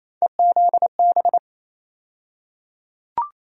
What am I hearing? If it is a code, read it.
Morse code, heard as a keyed tone, about 28 wpm, 700 Hz. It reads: EZB
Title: USA Call Sign Postfixes - 3 Letters 28wpm